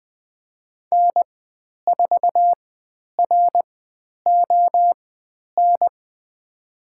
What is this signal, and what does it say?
Morse code, heard as a keyed tone, about 20 wpm, spaced as for 10 wpm, 700 Hz. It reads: N4RON
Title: 095 Lesson 16g - Letter c - Call Signs 20wpm